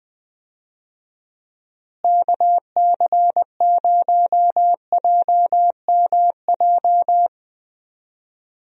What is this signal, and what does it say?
Morse code, heard as a keyed tone, about 20 wpm, 700 Hz. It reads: KC0JMJ
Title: USA Call Signs - 6 Characters - Repeated 3x - 20wpm